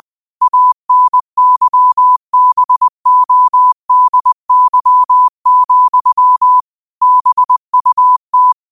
Morse code, heard as a keyed tone, about 20 wpm, 1000 Hz. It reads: ANYBODY, BUT
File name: qotdfiles / 20wpm